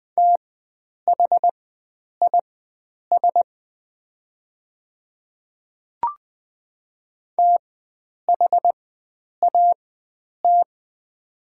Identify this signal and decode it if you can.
Morse code, heard as a keyed tone, about 20 wpm, spaced as for 9 wpm, 700 Hz. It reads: THIS THAT
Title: 4x Character Spacing - Top 100 Words - 20wpm